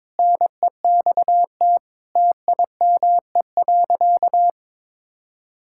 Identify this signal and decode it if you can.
Morse code, heard as a keyed tone, about 22 wpm, 700 Hz. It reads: NEXT TIME.
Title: Sentences from Top 1600 Words 22wpm